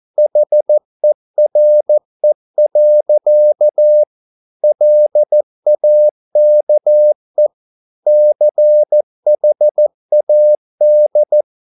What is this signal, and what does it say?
Morse code, heard as a keyed tone, about 14 wpm, 600 Hz. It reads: HERE. LAKE CHAD